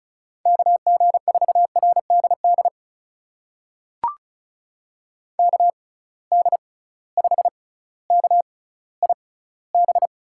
Morse code, heard as a keyed tone, about 36 wpm, spaced as for 18 wpm, 700 Hz. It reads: KG4RDD KD5KIB
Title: USA Call Signs - 6 Characters - 6x Character Spacing - 35wpm